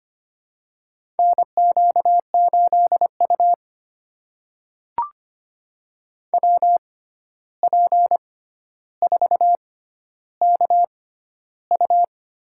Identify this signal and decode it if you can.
Morse code, heard as a keyed tone, about 25 wpm, spaced as for 12 wpm, 700 Hz. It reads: NQ8U WP4KU
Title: USA Call Signs - 6x Character Spacing - 25wpm